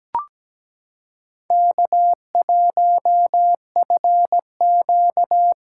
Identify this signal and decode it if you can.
Morse code, heard as a keyed tone, about 17 wpm, 700 Hz. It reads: K1FQ
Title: USA Callsign DE Callsign - 4 Characters - 17wpm